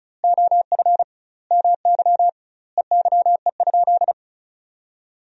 Morse code, heard as a keyed tone, about 34 wpm, 700 Hz. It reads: OF MY EYE?
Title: 2x Word-Spacing - Sentences from Top 500 Words - 35wpm